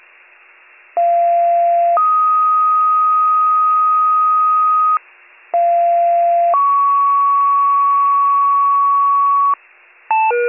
Two-tone page, heard as a one-second tone, then a three-second tone, about 690 then 1250 Hz, about 690 then 1120 Hz, with noise. It is heard from 1.0 to 9.5 s.